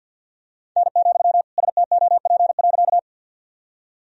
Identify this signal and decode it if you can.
Morse code, heard as a keyed tone, about 52 wpm, 700 Hz. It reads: N, STOP.
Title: Sentences from Top 300 Words and Abbreviations 50wpm